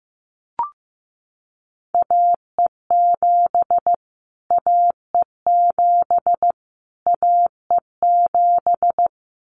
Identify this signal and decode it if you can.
Morse code, heard as a keyed tone, about 15 wpm, 700 Hz. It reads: AE7 AE7 AE7
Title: Callsign Prefixes for Top 25 DXCC - Repeated 3x - 15wpm